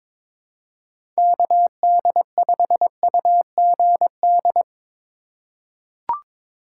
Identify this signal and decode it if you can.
Morse code, heard as a keyed tone, about 21 wpm, 700 Hz. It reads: KD5UGD
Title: USA Call Signs - 4x Character Spacing - 22wpm